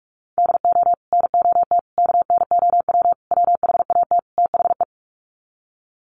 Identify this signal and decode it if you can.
Morse code, heard as a keyed tone, about 44 wpm, 700 Hz. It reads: DO NOT KNOW WHAT THE
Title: Sets of 5 Words 45wpm